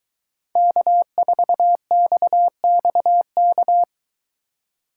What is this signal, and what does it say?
Morse code, heard as a keyed tone, about 22 wpm, 700 Hz. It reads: K4XXK